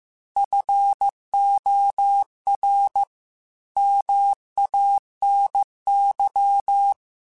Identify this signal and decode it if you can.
Morse code, heard as a keyed tone, about 15 wpm, 800 Hz. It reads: FOR MANY
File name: Technical Subjects / Technical Subjects-mp3s / AND Gate @13